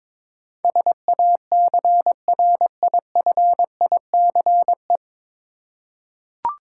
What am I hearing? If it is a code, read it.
Morse code, heard as a keyed tone, about 22 wpm, 700 Hz. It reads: SACRIFICE
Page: Morse Code Ninja practice 22wpm